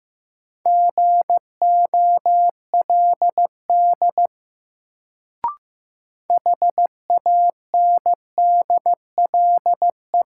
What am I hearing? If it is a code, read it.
Morse code, heard as a keyed tone, about 15 wpm, 700 Hz. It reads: GOLD HANDLE
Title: Sentences from Top 1100 Words - New Word Drill Alt - 15wpm